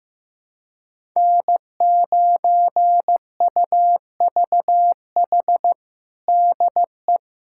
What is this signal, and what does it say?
Morse code, heard as a keyed tone, about 15 wpm, 700 Hz. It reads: N9UVH DE
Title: USA Callsign DE Callsign - 5 Characters - 15wpm